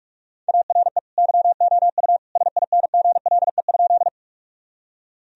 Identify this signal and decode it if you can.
Morse code, heard as a keyed tone, about 46 wpm, 700 Hz. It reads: ARE YOU SINGLE?
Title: Sentences from Top 1000 Words - Review Part 4 - 45wpm